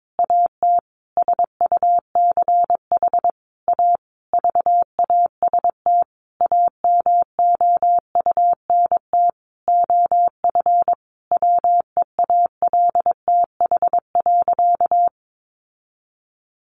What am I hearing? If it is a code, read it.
Morse code, heard as a keyed tone, about 22 wpm, 700 Hz. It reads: AT SUCH A VAST AMOUNT OF WEALTH.
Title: Sentences from Top 1300 Words 22wpm